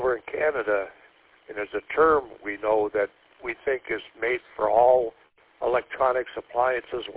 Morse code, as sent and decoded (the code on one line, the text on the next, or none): none